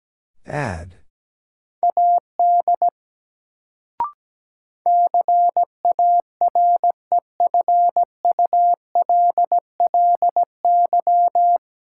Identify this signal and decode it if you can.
Morse code, heard as a keyed tone, about 17 wpm, 700 Hz. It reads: AD CAREFULLY
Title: Sentences from Top 1300 Words - New Word Drill - 17wpm